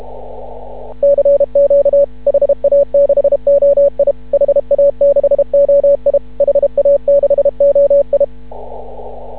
Morse code, heard as a keyed tone, about 33 wpm, 560 Hz. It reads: CQ HA6OI HA6OI HA6OI
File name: CW / ha6oi 2